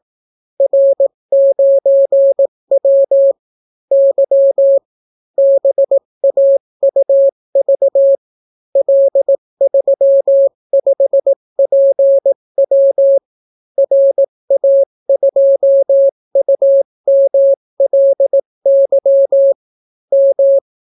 Morse code, heard as a keyed tone, about 18 wpm, 550 Hz. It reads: R9W Y BAUV L35PW RA2UMLY M